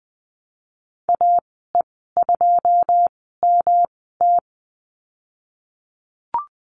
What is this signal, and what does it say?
Morse code, heard as a keyed tone, about 19 wpm, spaced as for 14 wpm, 700 Hz. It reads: AE2MT
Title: 140 Lesson 23g - Letter f - Call Signs 20wpm